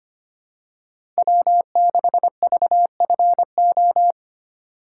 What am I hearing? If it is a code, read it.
Morse code, heard as a keyed tone, about 25 wpm, 700 Hz. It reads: W6VFO
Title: USA Call Signs - 5 Characters - 3x Character Spacing - 25wpm